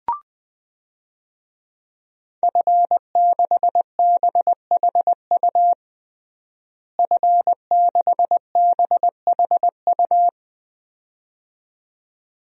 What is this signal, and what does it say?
Morse code, heard as a keyed tone, about 20 wpm, 700 Hz. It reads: F6BHU F6BHU
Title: DXCC-100 Call Signs - 5 Characters - Repeated 2x - 20wpm